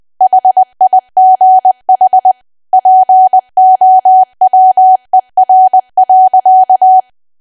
Morse code, heard as a keyed tone, about 20 wpm, 750 Hz. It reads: HIGH POWER.
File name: cwa / W1AW-Level3 / 20WPM